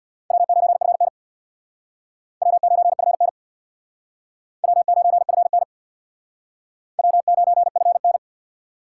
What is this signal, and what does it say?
Morse code, heard as a keyed tone, about 61 wpm, 700 Hz. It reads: W9FN W9FN W9FN W9FN
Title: Speed Racing - USA Call Signs 50wpm